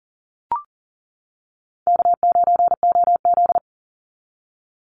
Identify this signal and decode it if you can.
Morse code, heard as a keyed tone, about 40 wpm, 700 Hz. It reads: K9OZ
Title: NA SOTA Chaser Callsigns 40wpm